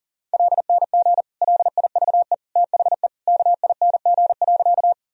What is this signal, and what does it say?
Morse code, heard as a keyed tone, about 40 wpm, 700 Hz. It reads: LNG LIVE THE KING.
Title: Sentences from Top 400 Words and Abbreviations 40wpm